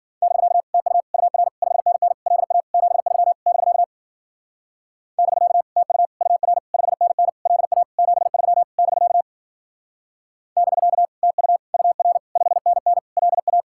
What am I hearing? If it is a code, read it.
Morse code, heard as a keyed tone, about 52 wpm, 700 Hz. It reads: <BK> TU UR 5NN LA 73 <BK> <BK> TU UR 5NN LA 73 <BK> <BK> TU UR 5NN LA